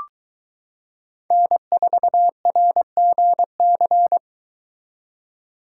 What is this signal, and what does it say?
Morse code, heard as a keyed tone, about 23 wpm, 700 Hz. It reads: N4RGC